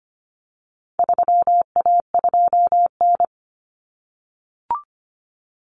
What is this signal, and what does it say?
Morse code, heard as a keyed tone, about 25 wpm, 700 Hz. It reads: 3A2N
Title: Common Cut Numbers - 4-Digit 25wpm